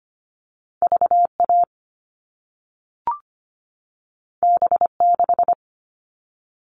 Morse code, heard as a keyed tone, about 25 wpm, 700 Hz. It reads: VA B6
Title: All Cut Numbers - 2-Digit 25wpm